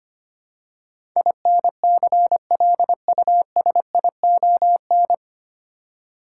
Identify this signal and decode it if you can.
Morse code, heard as a keyed tone, about 25 wpm, 700 Hz. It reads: INCLUSION
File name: practice / 25wpm / 9 Letter Words - First 3 Letters Given - 25wpm